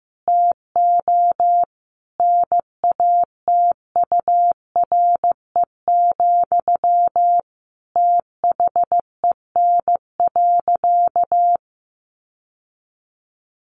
Morse code, heard as a keyed tone, about 15 wpm, 700 Hz. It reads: TO NATURE, THEN.
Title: Sentences from Top 1200 Words 15wpm